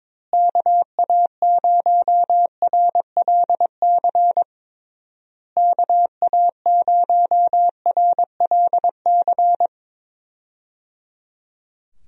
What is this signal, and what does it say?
Morse code, heard as a keyed tone, about 22 wpm, 700 Hz. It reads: KA0RLC KA0RLC
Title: USA Call Signs - 6 Characters - Repeated 3x - 22wpm